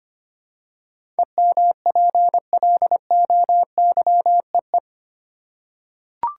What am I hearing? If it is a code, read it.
Morse code, heard as a keyed tone, about 25 wpm, 700 Hz. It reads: EMPLOYEE
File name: 8 Letter Words - First 3 Letters Given - 25wpm